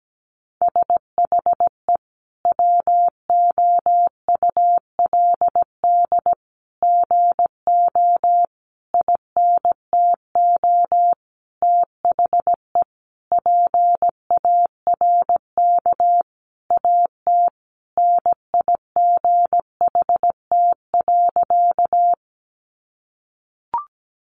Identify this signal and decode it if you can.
Morse code, heard as a keyed tone, about 17 wpm, 700 Hz. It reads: SHE WOULD GO INTO THE PARK AT NIGHT.
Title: Sentences from Top 1300 Words 17wpm